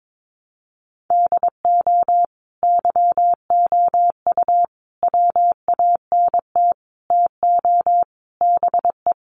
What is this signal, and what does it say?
Morse code, heard as a keyed tone, about 22 wpm, 700 Hz. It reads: DO YOU WANT TO BE
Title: Sets of 5 Words Encore2 22wpm